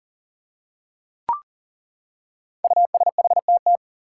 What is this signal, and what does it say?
Morse code, heard as a keyed tone, about 40 wpm, 700 Hz. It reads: USHTT
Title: Five Challenging Letter-Number Combinations 40wpm